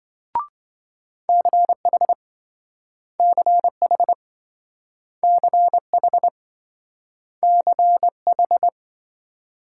Morse code, heard as a keyed tone, about 27 wpm, 700 Hz. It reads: CH CH CH CH